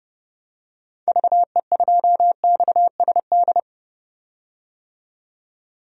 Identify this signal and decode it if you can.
Morse code, heard as a keyed tone, about 30 wpm, 700 Hz. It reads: VE2XSD
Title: DXCC-100 Call Signs - 6 Characters - 30wpm